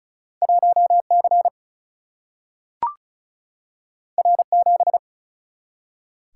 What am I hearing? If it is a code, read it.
Morse code, heard as a keyed tone, about 36 wpm, 700 Hz. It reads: 1 C R 7
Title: Single-Letter Single-Number 35wpm